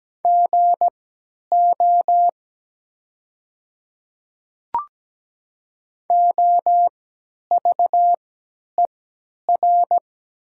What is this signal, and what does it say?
Morse code, heard as a keyed tone, about 17 wpm, spaced as for 10 wpm, 700 Hz. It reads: GO OVER
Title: 3x Character Spacing - Top 200 Words - 17wpm